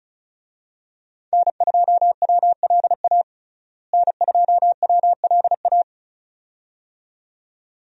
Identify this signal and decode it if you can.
Morse code, heard as a keyed tone, about 36 wpm, 700 Hz. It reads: N2WLA N2WLA
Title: USA Call Signs - 5 Characters - Repeated 2x - 35wpm